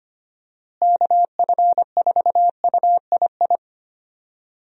Morse code, heard as a keyed tone, about 25 wpm, 700 Hz. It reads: KF4UII
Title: USA Call Signs - 6 Characters - Repeated 2x - 25wpm